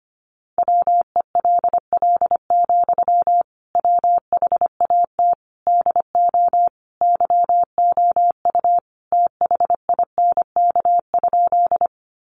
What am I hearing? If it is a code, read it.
Morse code, heard as a keyed tone, about 25 wpm, 700 Hz. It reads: WELL, WHAT DO YOU THINK?
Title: Sentences from Top 200 Words 25wpm